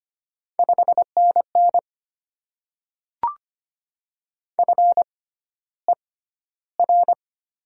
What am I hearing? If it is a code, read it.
Morse code, heard as a keyed tone, about 25 wpm, spaced as for 13 wpm, 700 Hz. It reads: 5NN FER
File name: Top 20 QSO Elements - 6x Character Spacing - 25wpm